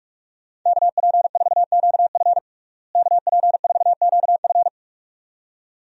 Morse code, heard as a keyed tone, about 46 wpm, 700 Hz. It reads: KP4QF KP4QF
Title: DXCC-100 Call Signs - 5 Characters - Repeated 2x - 45wpm